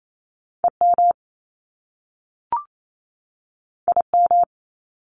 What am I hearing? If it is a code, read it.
Morse code, heard as a keyed tone, about 28 wpm, 700 Hz. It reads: EM IM
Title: Top Two Letter Combinations in Words 28wpm